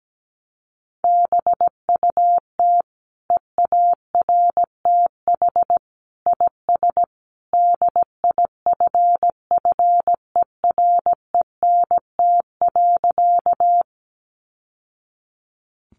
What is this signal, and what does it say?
Morse code, heard as a keyed tone, about 17 wpm, 700 Hz. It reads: BUT EARTH IS DIFFERENT.
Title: Sentences from Top 600 Words 17wpm